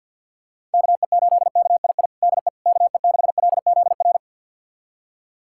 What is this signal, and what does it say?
Morse code, heard as a keyed tone, about 52 wpm, 700 Hz. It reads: KE8KII DE KE6LZR
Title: USA Callsign DE Callsign - 6 Characters - 50wpm